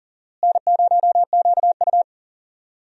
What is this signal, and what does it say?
Morse code, heard as a keyed tone, about 37 wpm, 700 Hz. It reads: N0QU
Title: USA Call Signs - 4 Characters - Repeated 2x - 40wpm